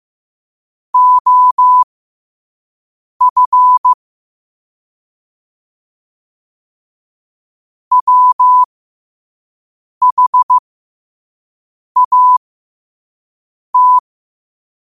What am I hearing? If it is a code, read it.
Morse code, heard as a keyed tone, about 15 wpm, spaced as for 5 wpm, 1000 Hz. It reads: OF WHAT
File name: qotdfiles / 5wpm